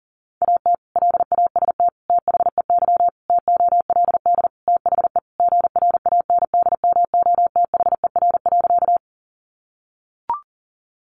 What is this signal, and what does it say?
Morse code, heard as a keyed tone, about 40 wpm, 700 Hz. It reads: AT LAST THEY TOLD THE GRANDMOTHER.
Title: Sentences from Top 1200 Words 40wpm